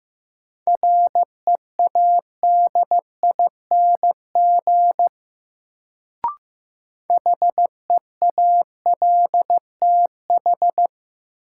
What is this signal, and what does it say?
Morse code, heard as a keyed tone, about 15 wpm, 700 Hz. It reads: READING HEALTH
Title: Sentences from Top 600 Words - New Word Drill - 15wpm